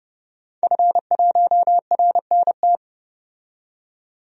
Morse code, heard as a keyed tone, about 30 wpm, 700 Hz. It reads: F1RNT
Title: DXCC-100 Call Signs - 5 Characters - 30wpm